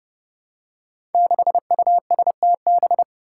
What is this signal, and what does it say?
Morse code, heard as a keyed tone, about 31 wpm, 700 Hz. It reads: 6USTB
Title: Five Challenging Letter-Number Combinations 30wpm